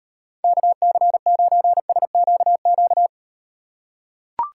Morse code, heard as a keyed tone, about 38 wpm, 700 Hz. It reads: KC9SQQ